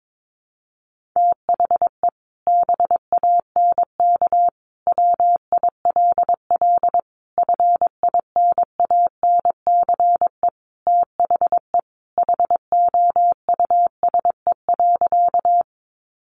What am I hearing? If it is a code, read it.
Morse code, heard as a keyed tone, about 22 wpm, 700 Hz. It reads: THE BANK WILL FINANCE THE HOUSE.
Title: Sentences from Top 1500 Words 22wpm